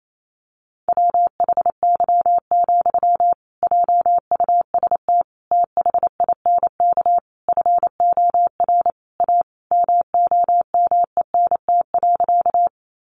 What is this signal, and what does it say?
Morse code, heard as a keyed tone, about 28 wpm, 700 Hz. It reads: WHY, JUST THINK FOR A MOMENT.
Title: Sentences from Top 1000 Words 28wpm